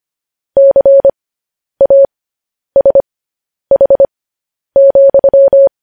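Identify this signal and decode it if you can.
Morse code, heard as a keyed tone, about 25 wpm, spaced as for 10 wpm, 550 Hz. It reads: CASH,
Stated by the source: Morse code news headlines sent at 10 words per minute.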